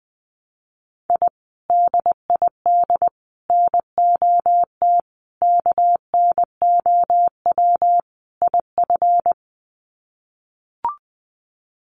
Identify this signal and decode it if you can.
Morse code, heard as a keyed tone, about 20 wpm, 700 Hz. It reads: I DID NOT KNOW IF
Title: Sets of 5 Words Encore 20wpm